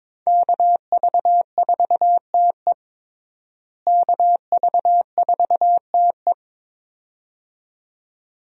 Morse code, heard as a keyed tone, about 22 wpm, 700 Hz. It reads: KV4TE KV4TE
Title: USA Call Signs - 5 Characters - Repeated 2x - 22wpm